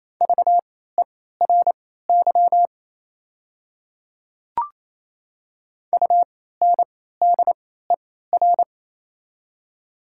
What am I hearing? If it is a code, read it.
Morse code, heard as a keyed tone, about 28 wpm, spaced as for 16 wpm, 700 Hz. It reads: VERY UNDER
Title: 3x Character Spacing - Top 200 Words - 28wpm